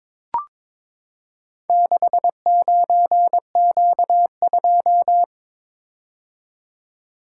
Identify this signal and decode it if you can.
Morse code, heard as a keyed tone, about 22 wpm, 700 Hz. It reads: 69Q2
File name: practice / 22wpm / Four Letter-Number Combinations - 22wpm